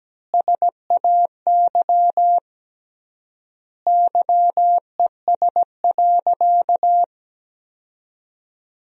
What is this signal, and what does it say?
Morse code, heard as a keyed tone, about 17 wpm, 700 Hz. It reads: SAY YES.
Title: 3x Word-Spacing - Sentences from Top 500 Words - 17wpm